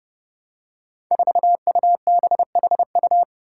Morse code, heard as a keyed tone, about 30 wpm, 700 Hz. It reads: 4UBHU
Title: Five Challenging Letter-Number Combinations 30wpm